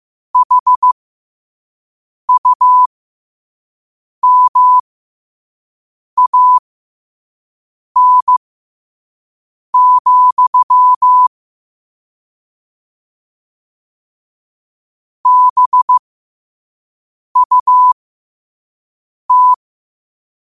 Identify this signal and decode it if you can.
Morse code, heard as a keyed tone, about 15 wpm, spaced as for 5 wpm, 1000 Hz. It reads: HUMAN, BUT
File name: qotdfiles / 5wpm